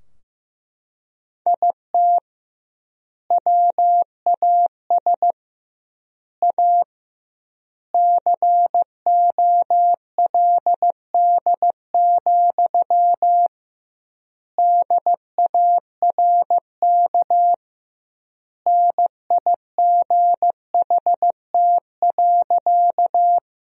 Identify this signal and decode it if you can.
Morse code, heard as a keyed tone, about 15 wpm, 700 Hz. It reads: IT WAS A COLD, DARK NIGHT.
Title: 2x Word-Spacing - Sentences from Top 500 Words - 15wpm